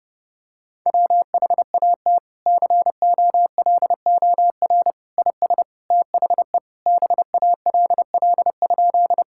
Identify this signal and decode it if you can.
Morse code, heard as a keyed tone, about 32 wpm, 700 Hz. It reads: WHAT COLOR IS THE BALL?
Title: Sentences from Top 1000 Words - Review Part 6 - 30wpm